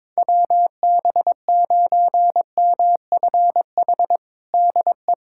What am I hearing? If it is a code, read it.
Morse code, heard as a keyed tone, about 21 wpm, 700 Hz. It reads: WB9MFH DE
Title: USA Callsign DE Callsign - 6 Characters - 22wpm